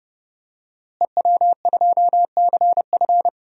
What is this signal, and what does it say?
Morse code, heard as a keyed tone, about 32 wpm, 700 Hz. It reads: EW2CF
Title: DXCC-100 Call Signs - 5 Characters - Repeated 3x - 30wpm